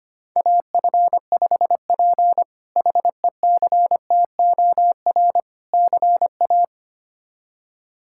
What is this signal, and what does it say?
Morse code, heard as a keyed tone, about 25 wpm, 700 Hz. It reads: AF5P HECTOR CA
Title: K1USN SST QSO Practice - 25wpm